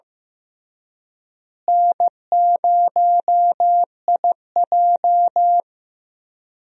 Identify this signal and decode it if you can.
Morse code, heard as a keyed tone, about 15 wpm, 700 Hz. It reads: N0IJ
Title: USA Call Signs - 4 Characters - Repeated 3x - 15wpm